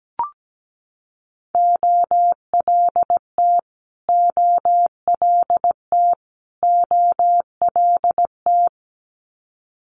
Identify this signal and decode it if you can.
Morse code, heard as a keyed tone, about 17 wpm, 700 Hz. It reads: OLT OLT OLT
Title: Multi-letter Phonemes Repeated 3x 17wpm